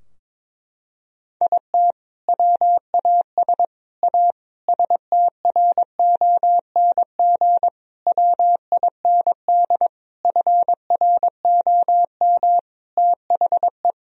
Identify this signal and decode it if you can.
Morse code, heard as a keyed tone, about 22 wpm, 700 Hz. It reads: IT WAS A STRONG WIND FROM THE
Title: Sentences from Top 1000 Words - Review Part 2 - 22wpm